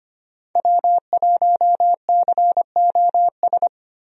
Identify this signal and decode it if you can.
Morse code, heard as a keyed tone, about 25 wpm, 700 Hz. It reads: W1COS
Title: USA Call Signs - 5 Characters - 25wpm